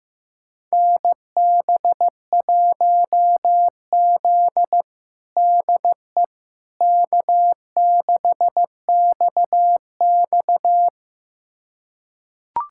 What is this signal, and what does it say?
Morse code, heard as a keyed tone, about 14 wpm, 700 Hz. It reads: NB1Z DE K6XX
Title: USA Callsign DE Callsign - 4 Characters - 15wpm